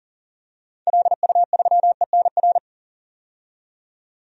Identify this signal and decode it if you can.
Morse code, heard as a keyed tone, about 40 wpm, 700 Hz. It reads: LU3ENR
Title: DXCC-100 Call Signs - 6 Characters - 40wpm